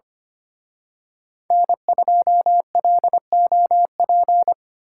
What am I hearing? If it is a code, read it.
Morse code, heard as a keyed tone, about 29 wpm, 700 Hz. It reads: N2LOP
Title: Speed Racing - USA Call Signs 25wpm